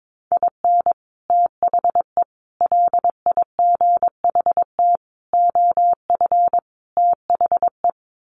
Morse code, heard as a keyed tone, about 22 wpm, 700 Hz. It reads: IN THE LIGHT OF THE
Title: Sets of 5 Words Encore2 22wpm